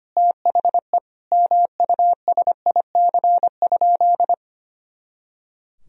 Morse code, heard as a keyed tone, about 25 wpm, 700 Hz. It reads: THE MUSIC?
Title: Sentences from Top 300 Words and Abbreviations 25wpm